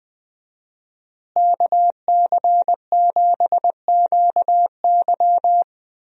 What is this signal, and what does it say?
Morse code, heard as a keyed tone, about 20 wpm, 700 Hz. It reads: KC7QY
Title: USA Call Signs - 5 Characters - Repeated 2x - 20wpm